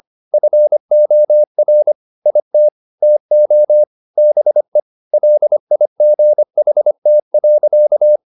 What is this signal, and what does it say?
Morse code, heard as a keyed tone, about 25 wpm, 600 Hz. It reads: FOR IT TO BE LIGHT.
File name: the old man and the sea part 1 25wpm